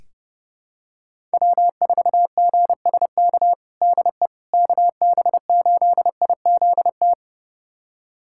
Morse code, heard as a keyed tone, about 32 wpm, 700 Hz. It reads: W4GSK DE KB8IZT
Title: USA Callsign DE Callsign 30wpm